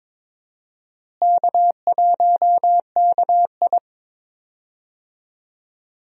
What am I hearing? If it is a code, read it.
Morse code, heard as a keyed tone, about 22 wpm, 700 Hz. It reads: K1KI
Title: USA Call Signs - 4 Characters - Repeated 2x - 22wpm